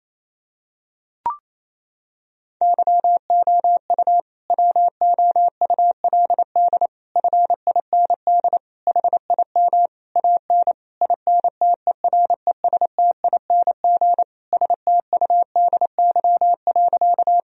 Morse code, heard as a keyed tone, about 28 wpm, 700 Hz. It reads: YOU WOULD FIND HIM AN INTERESTING STUDY.
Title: Sentences from Top 1000 Words - Review Part 1 - 28wpm